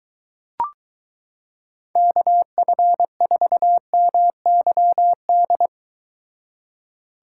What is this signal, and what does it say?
Morse code, heard as a keyed tone, about 23 wpm, 700 Hz. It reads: KF4MYD